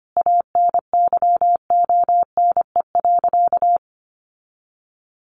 Morse code, heard as a keyed tone, about 25 wpm, 700 Hz. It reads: ANYONE.
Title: Sentences from Top 900 Words 25wpm